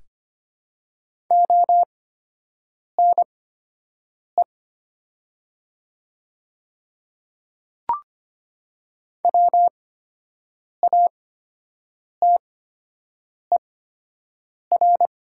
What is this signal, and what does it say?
Morse code, heard as a keyed tone, about 25 wpm, spaced as for 7 wpm, 700 Hz. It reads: ONE WATER
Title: 8x Character Spacing - Top 100 Words - 25wpm